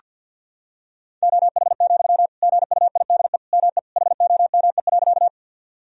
Morse code, heard as a keyed tone, about 46 wpm, 700 Hz. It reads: OH, GUIDE ME HOME.
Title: Sentences from Top 1000 Words - Review Part 3 - 50wpm